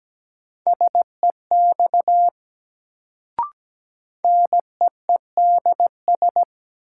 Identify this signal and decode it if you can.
Morse code, heard as a keyed tone, about 17 wpm, 700 Hz. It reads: SEX NEEDS
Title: Sentences from Top 1600 Words - New Word Drill Alt - 17wpm